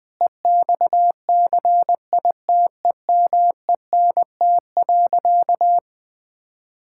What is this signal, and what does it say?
Morse code, heard as a keyed tone, about 20 wpm, 700 Hz. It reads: EXCITEMENT.
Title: Difficult Sentences with Hints - 20wpm